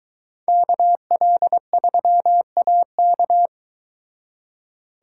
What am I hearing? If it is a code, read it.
Morse code, heard as a keyed tone, about 22 wpm, 700 Hz. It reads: KL3AK